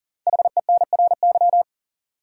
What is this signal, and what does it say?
Morse code, heard as a keyed tone, about 37 wpm, 700 Hz. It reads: HENRY
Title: USA Common Names - 40wpm